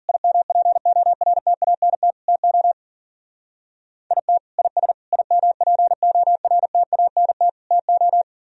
Morse code, heard as a keyed tone, about 50 wpm, 700 Hz. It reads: IMPORTANT TO IT IS IMPORTANT TO